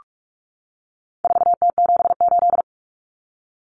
Morse code, heard as a keyed tone, about 44 wpm, 700 Hz. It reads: 4T78
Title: Common Cut Numbers - 4-Digit 45wpm